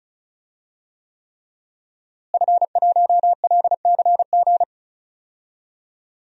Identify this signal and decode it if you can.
Morse code, heard as a keyed tone, about 36 wpm, 700 Hz. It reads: F1LCG
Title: DXCC-100 Call Signs - 5 Characters - Repeated 3x - 35wpm